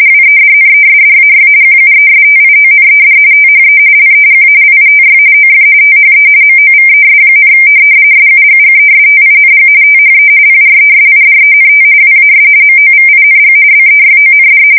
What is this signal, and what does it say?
RTTY, about 45.45 baud. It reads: WEDNESDAY QUESTIONED THE LEGALITY OF THE MOVES, SUGGESTING THE PRESIDENT WAS MOVING BEYOND
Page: 170 Shift 2125/2295